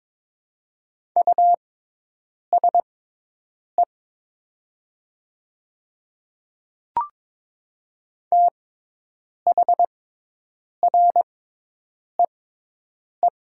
Morse code, heard as a keyed tone, about 22 wpm, spaced as for 8 wpm, 700 Hz. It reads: USE THREE
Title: 6x Character Spacing - Top 300 Words - 22wpm